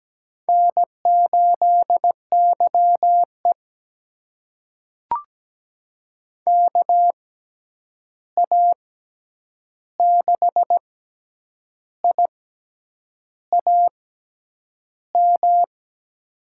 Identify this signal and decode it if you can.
Morse code, heard as a keyed tone, about 17 wpm, spaced as for 8 wpm, 700 Hz. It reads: N8YE KA6IAM
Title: USA Call Signs - 6x Character Spacing - 17wpm